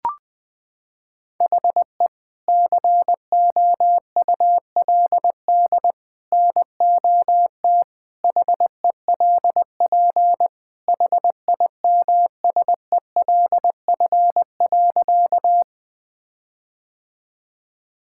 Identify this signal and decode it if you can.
Morse code, heard as a keyed tone, about 20 wpm, 700 Hz. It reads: HE COULD NOT HELP HIMSELF.
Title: Sentences from Top 1600 Words 20wpm